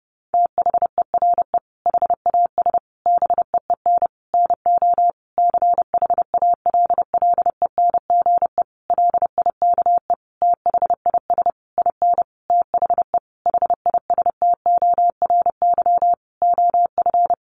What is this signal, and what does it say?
Morse code, heard as a keyed tone, about 30 wpm, 700 Hz. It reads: THERE HAS BEEN NO CHALLENGE LIKE THIS IN THE HISTORY OF
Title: Sentences from Top 1500 Words 30wpm